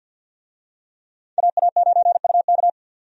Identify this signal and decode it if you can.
Morse code, heard as a keyed tone, about 51 wpm, 700 Hz. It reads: AA9UK
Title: USA Call Signs - 5 Characters - 50wpm